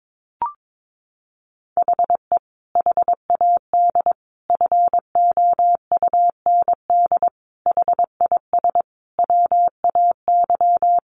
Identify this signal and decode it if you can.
Morse code, heard as a keyed tone, about 22 wpm, 700 Hz. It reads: HE HAD FOUND HIS WAY
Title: A Sound Of Thunder 22wpm